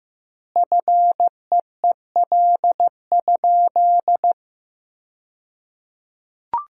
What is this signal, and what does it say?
Morse code, heard as a keyed tone, about 16 wpm, 700 Hz. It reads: FEEL?
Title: 2x Word-Spacing - Sentences from Top 400 Words - 15wpm